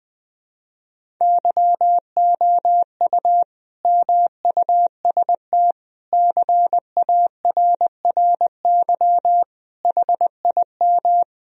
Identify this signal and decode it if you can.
Morse code, heard as a keyed tone, about 20 wpm, 700 Hz. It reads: YOU MUST CARRY HIM
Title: Sentences from Top 1000 Words - Review Part 1 - 20wpm